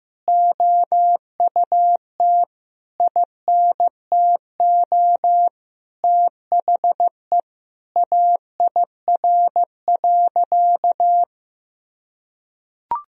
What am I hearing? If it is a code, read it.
Morse code, heard as a keyed tone, about 16 wpm, 700 Hz. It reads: OUT INTO THE AIR.
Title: Sentences from Top 1000 Words - Review Part 1 - 15wpm